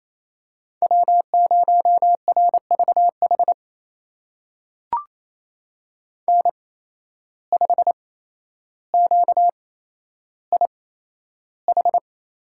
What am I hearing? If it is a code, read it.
Morse code, heard as a keyed tone, about 28 wpm, spaced as for 12 wpm, 700 Hz. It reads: W0RVH N5QIH
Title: USA Call Signs - 8x Character Spacing - 28wpm